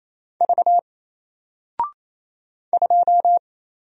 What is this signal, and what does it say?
Morse code, heard as a keyed tone, about 27 wpm, 700 Hz. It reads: V 2
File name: Single-Letter-Number-28wpm